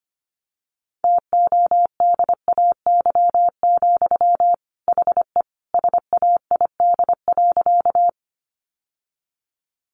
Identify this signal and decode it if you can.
Morse code, heard as a keyed tone, about 25 wpm, 700 Hz. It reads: TODAY, HE SAID.
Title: Sentences from Top 600 Words 25wpm